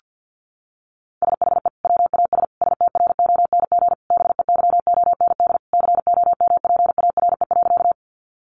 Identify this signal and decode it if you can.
Morse code, heard as a keyed tone, about 50 wpm, 700 Hz. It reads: SHE WAS STRONG BEYOND COMPARE.
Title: Sentences from Top 1300 Words 50wpm